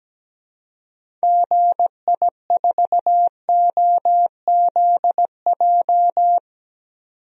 Morse code, heard as a keyed tone, about 17 wpm, 700 Hz. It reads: GI4OZJ
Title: DXCC-100 Call Signs - 6 Characters - 17wpm